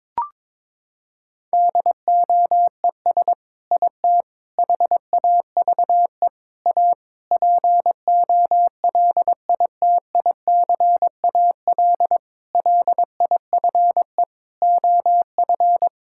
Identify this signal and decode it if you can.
Morse code, heard as a keyed tone, about 22 wpm, 700 Hz. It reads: DOES IT HAVE A POLITICAL LIFE OF
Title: Sentences from Top 1000 Words - Review Part 5 - 22wpm